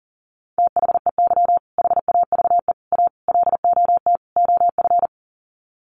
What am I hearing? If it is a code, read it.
Morse code, heard as a keyed tone, about 40 wpm, 700 Hz. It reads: THEY HAVE A LOT OF
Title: Sets of 5 Words Encore2 40wpm